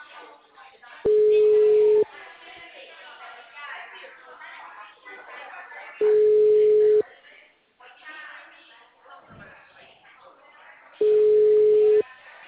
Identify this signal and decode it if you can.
Ringback tone (Europe, 425 Hz), with speech.
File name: out- -2009- - - .472